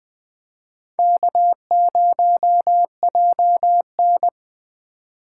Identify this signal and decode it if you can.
Morse code, heard as a keyed tone, about 20 wpm, 700 Hz. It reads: K0JN